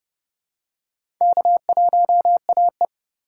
Morse code, heard as a keyed tone, about 30 wpm, 700 Hz. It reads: K1AE
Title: USA Call Signs - 4 Characters - 2x Character Spacing - 30wpm